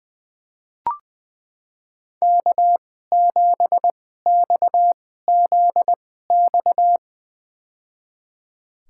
Morse code, heard as a keyed tone, about 20 wpm, spaced as for 14 wpm, 700 Hz. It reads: K7XZX
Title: USA Call Signs - 5 Characters - 2x Character Spacing - 20wpm